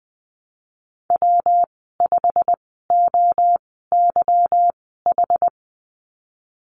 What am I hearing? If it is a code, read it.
Morse code, heard as a keyed tone, about 20 wpm, spaced as for 15 wpm, 700 Hz. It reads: W5OYH
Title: 193 Lesson 31g - Letter v - Call Signs 20wpm